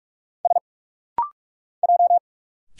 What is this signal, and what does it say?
Morse code, heard as a keyed tone, about 44 wpm, 700 Hz. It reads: S J
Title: Single Letter-Number - Rapid-Fire 45wpm